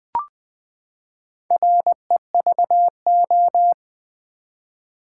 Morse code, heard as a keyed tone, about 20 wpm, 700 Hz. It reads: REVO
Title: Build A Word - 4 Letters At A Time - 10-Letter Words - 20wpm